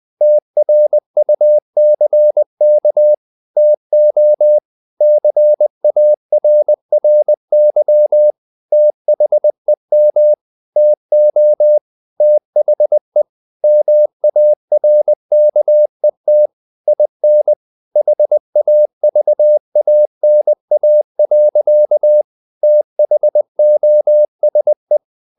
Morse code, heard as a keyed tone, about 20 wpm, 600 Hz. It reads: TRUCK TO CARRY THEM TO THE MARKET IN HAVANA. THOSE
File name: the old man and the sea / the old man and the sea part 1 20wpm